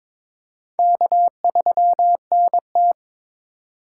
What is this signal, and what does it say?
Morse code, heard as a keyed tone, about 22 wpm, 700 Hz. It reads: K3NT
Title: USA Call Signs - 3x Character Spacing - 22wpm